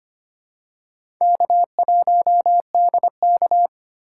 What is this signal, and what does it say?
Morse code, heard as a keyed tone, about 25 wpm, 700 Hz. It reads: K1DK
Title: USA Call Signs - 4 Characters - 4x Character Spacing - 25wpm